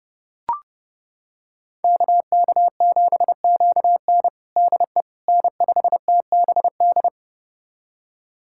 Morse code, heard as a keyed tone, about 30 wpm, 700 Hz. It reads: KK7QN DE N5TBD
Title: USA Callsign DE Callsign - 5 Characters - 30wpm